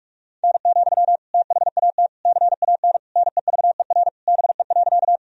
Morse code, heard as a keyed tone, about 42 wpm, 700 Hz. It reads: N, THAT CAN NEVER BE.
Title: Sentences from Top 300 Words and Abbreviations 45wpm